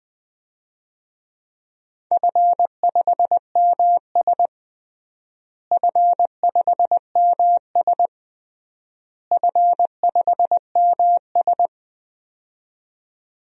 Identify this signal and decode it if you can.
Morse code, heard as a keyed tone, about 21 wpm, 700 Hz. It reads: F5MS F5MS F5MS
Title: DXCC-100 Call Signs - 4 Characters - Repeated 3x - 20wpm